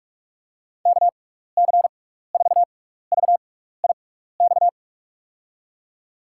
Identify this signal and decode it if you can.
Morse code, heard as a keyed tone, about 44 wpm, spaced as for 15 wpm, 700 Hz. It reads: KC4VIX
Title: USA Call Signs - 6x Character Spacing - 45wpm